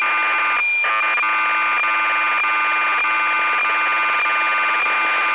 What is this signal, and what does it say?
Cassette data (Kansas City type).